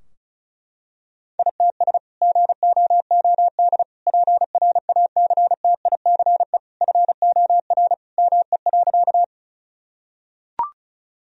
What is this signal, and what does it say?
Morse code, heard as a keyed tone, about 36 wpm, 700 Hz. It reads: ITS GOOD PRACTICE FOR ME.
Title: Sentences from Top 1000 Words - Review Part 4 - 35wpm